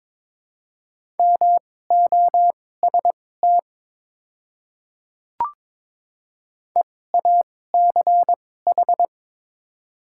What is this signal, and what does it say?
Morse code, heard as a keyed tone, about 22 wpm, spaced as for 16 wpm, 700 Hz. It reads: MOST EACH
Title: 2x Character Spacing - Top 300 Words - 22wpm